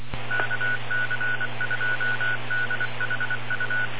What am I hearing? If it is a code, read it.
Morse code, heard as a keyed tone, about 25 wpm, 1480 Hz, with noise. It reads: KC2DHU